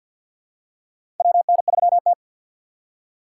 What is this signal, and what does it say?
Morse code, heard as a keyed tone, about 52 wpm, 700 Hz. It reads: WN3T
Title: USA Call Signs - 4 Characters - 2x Character Spacing - 50wpm